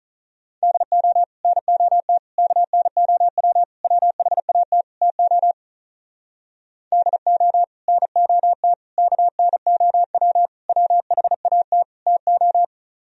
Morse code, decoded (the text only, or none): DO NOT KNOW WHAT TO DO NOT KNOW WHAT TO